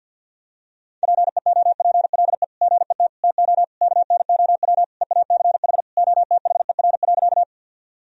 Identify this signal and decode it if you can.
Morse code, heard as a keyed tone, about 52 wpm, 700 Hz. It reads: PEOPLE GET TO KNOW EACH OTHER.
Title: Sentences from Top 1000 Words - Review Part 1 - 50wpm